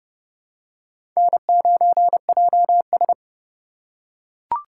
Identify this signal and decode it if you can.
Morse code, heard as a keyed tone, about 30 wpm, 700 Hz. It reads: N9JS
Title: USA Call Signs - 4 Characters - 4x Character Spacing - 30wpm